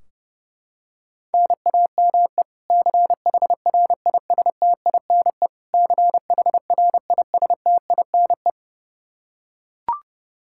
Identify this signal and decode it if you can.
Morse code, heard as a keyed tone, about 30 wpm, 700 Hz. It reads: NAME CHRISTINE CHRISTINE
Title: Practice QSO - Exchange 1 - Beginning - 30wpm